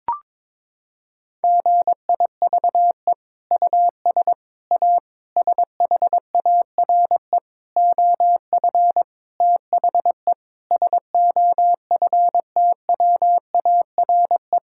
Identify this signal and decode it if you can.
Morse code, heard as a keyed tone, about 22 wpm, 700 Hz. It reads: GIVE US A SHARE OF THE SOFTWARE
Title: Sentences from Top 1000 Words - Review Part 6 - 22wpm